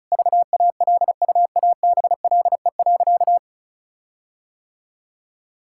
Morse code, heard as a keyed tone, about 36 wpm, 700 Hz. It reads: VALUABLE.
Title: Sentences from Top 1600 Words 35wpm